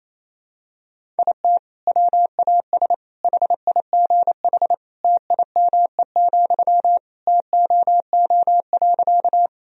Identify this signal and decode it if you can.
Morse code, heard as a keyed tone, about 28 wpm, 700 Hz. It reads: IT WAS HIGH TIME, TOO.
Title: Sentences from Top 1000 Words - Review Part 1 - 28wpm